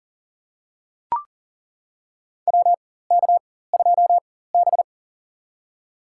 Morse code, heard as a keyed tone, about 40 wpm, spaced as for 19 wpm, 700 Hz. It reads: WK2B
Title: USA Call Signs - 4 Characters - 4x Character Spacing - 40wpm